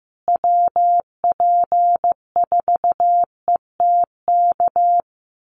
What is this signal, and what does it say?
Morse code, heard as a keyed tone, about 15 wpm, 700 Hz. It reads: WP4ETK
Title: USA Call Signs Repeated 2x 15wpm